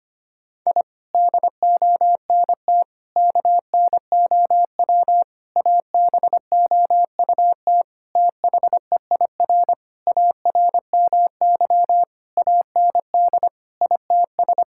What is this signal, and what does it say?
Morse code, heard as a keyed tone, about 25 wpm, 700 Hz. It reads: I DONT KNOW ABOUT THEIR ARMY AND ITS
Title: Sentences from Top 1000 Words - Review Part 5 - 25wpm